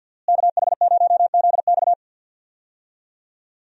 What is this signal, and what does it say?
Morse code, heard as a keyed tone, about 51 wpm, 700 Hz. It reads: KH0ZX
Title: USA Call Signs - 5 Characters - Repeated 3x - 50wpm